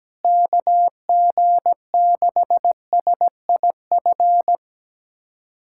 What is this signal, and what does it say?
Morse code, heard as a keyed tone, about 17 wpm, 700 Hz. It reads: KG6SIF
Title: USA Callsign DE Callsign 17wpm